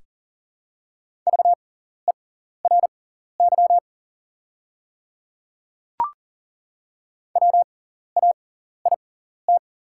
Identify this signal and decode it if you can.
Morse code, heard as a keyed tone, about 37 wpm, spaced as for 14 wpm, 690 Hz. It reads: VERY WAIT
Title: 6x Character Spacing - Top 500 Words - 40wpm